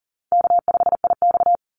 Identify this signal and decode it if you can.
Morse code, heard as a keyed tone, about 40 wpm, 700 Hz. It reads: K5IX
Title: Call Signs Repeated 3x - 40wpm